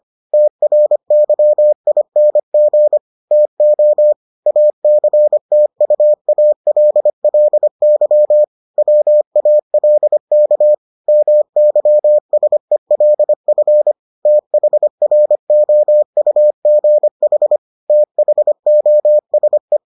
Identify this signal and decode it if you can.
Morse code, heard as a keyed tone, about 25 wpm, 600 Hz. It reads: TRYING TO ACTUALLY WALK MYSELF THROUGH THOSE